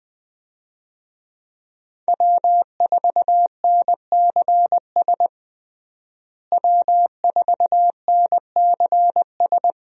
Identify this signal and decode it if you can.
Morse code, heard as a keyed tone, about 20 wpm, 700 Hz. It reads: W4NCS W4NCS
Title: USA Call Signs - 5 Characters - Repeated 3x - 20wpm